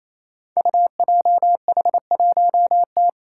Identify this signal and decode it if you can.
Morse code, heard as a keyed tone, about 29 wpm, 700 Hz. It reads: UJH1T
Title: Five Challenging Letter-Number Combinations 28wpm